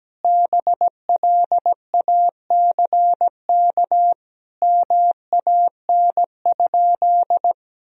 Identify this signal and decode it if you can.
Morse code, heard as a keyed tone, about 17 wpm, 700 Hz. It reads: BLACK MAN?
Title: Sentences from Top 1000 Words - Review Part 2 - 17wpm